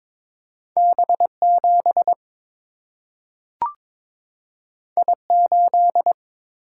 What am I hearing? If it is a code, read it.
Morse code, heard as a keyed tone, about 22 wpm, 700 Hz. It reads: B 7 I 8
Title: Single-Letter Single-Number 22wpm